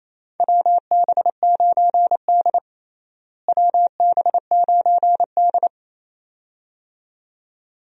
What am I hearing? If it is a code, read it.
Morse code, heard as a keyed tone, about 28 wpm, 700 Hz. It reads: WB9D WB9D
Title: USA Call Signs - 4 Characters - Repeated 2x - 28wpm